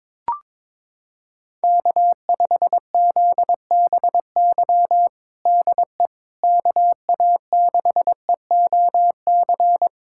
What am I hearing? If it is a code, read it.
Morse code, heard as a keyed tone, about 22 wpm, 700 Hz. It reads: K5ZBY DE KA6EOC
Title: USA Callsign DE Callsign 22wpm